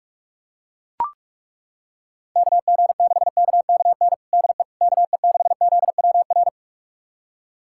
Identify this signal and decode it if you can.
Morse code, heard as a keyed tone, about 42 wpm, 700 Hz. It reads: KG6KKN DE KE6ZWR
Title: USA Callsign DE Callsign - 6 Characters - 45wpm